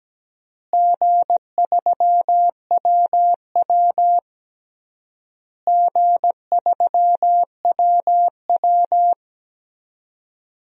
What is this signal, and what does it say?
Morse code, heard as a keyed tone, about 17 wpm, 700 Hz. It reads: G3WW G3WW
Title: DXCC-100 Call Signs - 4 Characters - Repeated 3x - 17wpm